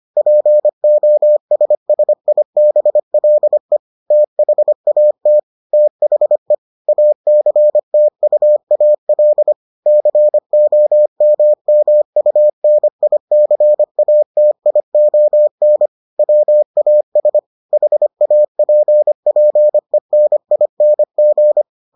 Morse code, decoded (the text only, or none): POSSIBLE THAT THE ACTUAL COMMUNICATION WAS HAPPENING